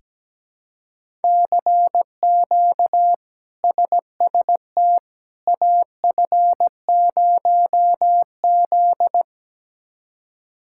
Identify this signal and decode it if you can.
Morse code, heard as a keyed tone, about 17 wpm, 700 Hz. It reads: CQ SST AF0Z
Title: K1USN SST QSO Practice - 17wpm